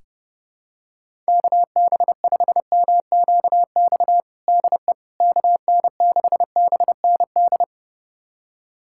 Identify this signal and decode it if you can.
Morse code, heard as a keyed tone, about 31 wpm, 700 Hz. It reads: KB5MQX DE KN6BND
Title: USA Callsign DE Callsign 30wpm